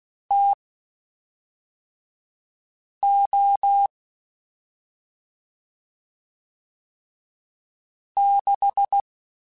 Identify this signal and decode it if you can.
Morse code, heard as a keyed tone, about 16 wpm, spaced as for 3 wpm, 800 Hz. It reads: TO 6
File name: Spaced Characters / Spaced Characters-mp3s / Weather 4 10 sp @5